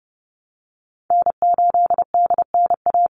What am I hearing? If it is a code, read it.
Morse code, heard as a keyed tone, about 30 wpm, 700 Hz. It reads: N8DNA
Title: Call Signs Repeated 3x - 30wpm